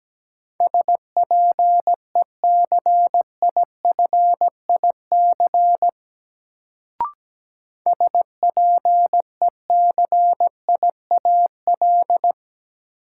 Morse code, heard as a keyed tone, about 17 wpm, 700 Hz. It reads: SPECIFIC SPECIAL
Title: Sentences from Top 1000 Words - Review Part 3 - New Word Drill - 17wpm